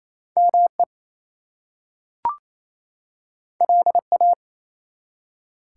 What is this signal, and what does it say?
Morse code, heard as a keyed tone, about 28 wpm, 700 Hz. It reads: ME LA
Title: US States and Canadian Provinces 28wpm